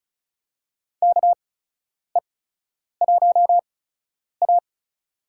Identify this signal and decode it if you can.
Morse code, heard as a keyed tone, about 34 wpm, spaced as for 10 wpm, 700 Hz. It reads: KE1A
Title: USA Call Signs - 4 Characters - 8x Character Spacing - 35wpm